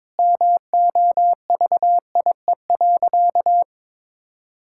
Morse code, heard as a keyed tone, about 22 wpm, 700 Hz. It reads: MOVIE.
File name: practice / 22wpm / Sentences with 3-Word Phrases - 22wpm